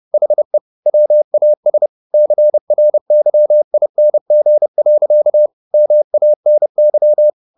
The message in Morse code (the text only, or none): HE WAS CRYING. MANY